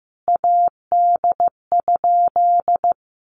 Morse code, heard as a keyed tone, about 15 wpm, 700 Hz. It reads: AD?
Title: Sentences from Top 1300 Words 15wpm